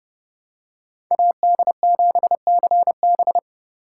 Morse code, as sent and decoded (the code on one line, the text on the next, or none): .- -.. --... -.-. -...
AD7CB